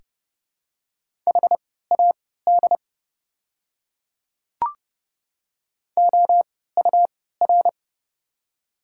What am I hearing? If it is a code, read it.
Morse code, heard as a keyed tone, about 29 wpm, spaced as for 17 wpm, 700 Hz. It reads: HAD OUR
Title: 3x Character Spacing - Top 200 Words - 30wpm